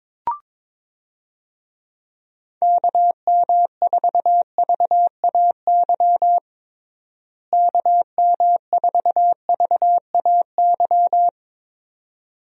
Morse code, heard as a keyed tone, about 22 wpm, 700 Hz. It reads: KM4VAY KM4VAY
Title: USA Call Signs - 6 Characters - Repeated 2x - 22wpm